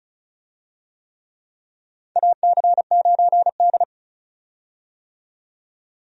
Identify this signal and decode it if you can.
Morse code, heard as a keyed tone, about 36 wpm, 700 Hz. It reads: AC9D
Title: USA Call Signs - 4 Characters - 35wpm